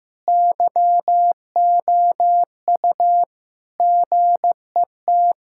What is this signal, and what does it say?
Morse code, heard as a keyed tone, about 15 wpm, 700 Hz. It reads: YOU GET
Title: Sentences from Top 1600 Words 15wpm